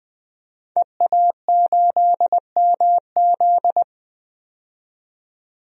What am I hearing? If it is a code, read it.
Morse code, heard as a keyed tone, about 20 wpm, 700 Hz. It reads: EA8MZ
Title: DXCC-100 Call Signs - 5 Characters - 20wpm